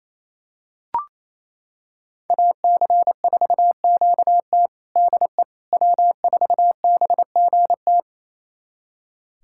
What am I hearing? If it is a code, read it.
Morse code, heard as a keyed tone, about 28 wpm, 700 Hz. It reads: AC4QT DE W4BGT
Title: USA Callsign DE Callsign - 5 Characters - 28wpm